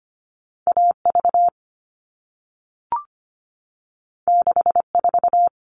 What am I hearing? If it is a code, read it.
Morse code, heard as a keyed tone, about 25 wpm, 700 Hz. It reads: AV 64
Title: All Cut Numbers - 2-Digit 25wpm